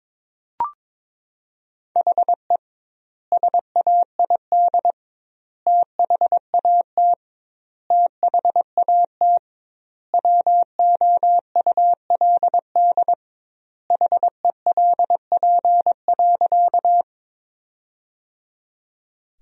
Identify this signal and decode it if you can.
Morse code, heard as a keyed tone, about 22 wpm, 700 Hz. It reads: HE SAID THAT THAT WOULD HELP.
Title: 2x Word-Spacing - Sentences from Top 200 Words - 22wpm